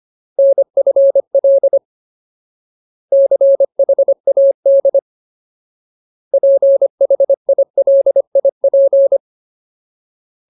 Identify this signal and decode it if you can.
Morse code, heard as a keyed tone, about 25 wpm, 550 Hz. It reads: NFL CHAD PHILIP
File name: Ultimate Word List / 25WPM Ultimate